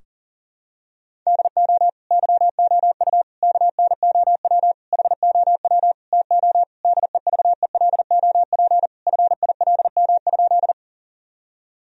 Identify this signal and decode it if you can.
Morse code, heard as a keyed tone, about 40 wpm, 700 Hz. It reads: DO YOU KNOW HOW TO DEVELOP FILM?
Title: Sentences from Top 1000 Words - Review Part 3 - 40wpm